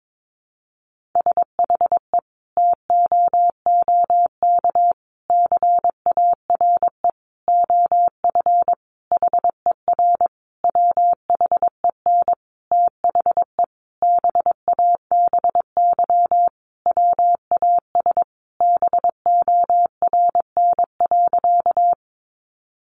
Morse code, heard as a keyed tone, about 22 wpm, 700 Hz. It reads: SHE TOOK CARE OF HER WHEN THE BABY WAS BORN.
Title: Sentences from Top 1300 Words 22wpm